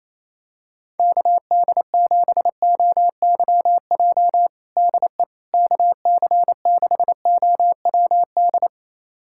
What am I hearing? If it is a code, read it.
Morse code, heard as a keyed tone, about 28 wpm, 700 Hz. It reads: KD7OYJ DE KC6OWD
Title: USA Callsign DE Callsign - 6 Characters - 28wpm